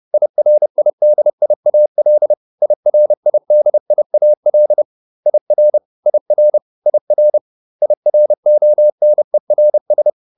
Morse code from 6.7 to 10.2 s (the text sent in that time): IR IRONERS